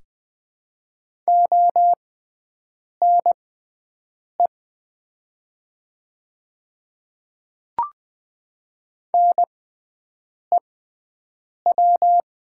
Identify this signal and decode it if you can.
Morse code, heard as a keyed tone, about 21 wpm, spaced as for 7 wpm, 700 Hz. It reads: ONE NEW
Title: 6x Character Spacing - Top 200 Words - 20wpm